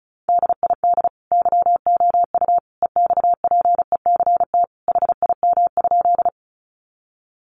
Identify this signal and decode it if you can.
Morse code, heard as a keyed tone, about 34 wpm, 700 Hz. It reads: DID YOU EXPECT HIM?
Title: Sentences from Top 1100 Words 35wpm